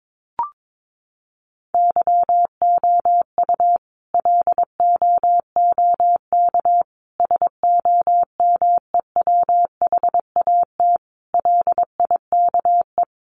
Morse code, heard as a keyed tone, about 22 wpm, 700 Hz. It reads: YOU LOOK SOMEWHAT LIKE
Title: Sentences from Top 1300 Words 22wpm